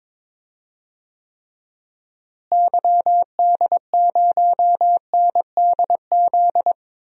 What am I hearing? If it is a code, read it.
Morse code, heard as a keyed tone, about 22 wpm, 700 Hz. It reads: YD0NDZ
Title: DXCC-100 Call Signs - 6 Characters - Repeated 3x - 22wpm